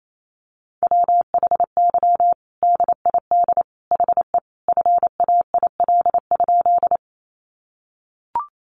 Morse code, heard as a keyed tone, about 28 wpm, 700 Hz. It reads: WHY DID HE FAIL?
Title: Sentences from Top 1100 Words 28wpm